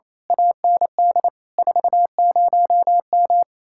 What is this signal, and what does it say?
Morse code, heard as a keyed tone, about 28 wpm, 700 Hz. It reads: AND 40M
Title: Speed-Racing Common QSO Phrases - 28wpm